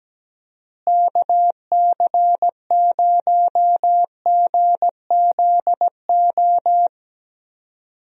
Morse code, heard as a keyed tone, about 17 wpm, 700 Hz. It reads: KC0GZO
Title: USA Call Signs - 6 Characters - Repeated 2x - 17wpm